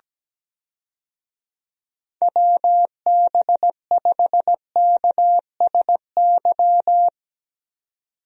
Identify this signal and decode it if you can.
Morse code, heard as a keyed tone, about 17 wpm, 700 Hz. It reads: WB5KSY
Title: USA Call Signs - 6 Characters - 17wpm